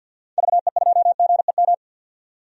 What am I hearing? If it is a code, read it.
Morse code, heard as a keyed tone, about 52 wpm, 700 Hz. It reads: VE2GEM
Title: DXCC-100 Call Signs - 6 Characters - 50wpm